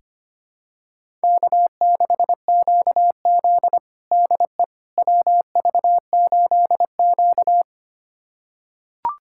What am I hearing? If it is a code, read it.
Morse code, heard as a keyed tone, about 25 wpm, 700 Hz. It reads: K6QZ DE WV8Q
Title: USA Callsign DE Callsign - 4 Characters - 25wpm